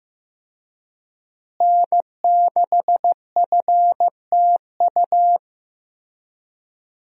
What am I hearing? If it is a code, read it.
Morse code, heard as a keyed tone, about 15 wpm, 700 Hz. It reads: N6FTU
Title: USA Call Signs - 5 Characters - Repeated 3x - 15wpm